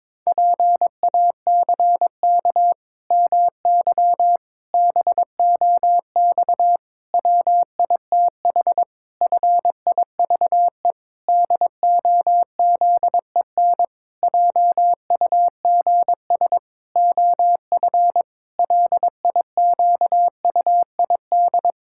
Morse code, heard as a keyed tone, about 22 wpm, 700 Hz. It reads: PACK MY BOX WITH FIVE DOZEN JUGS OF LIQUID
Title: Pangram Challenge v2 - 22wpm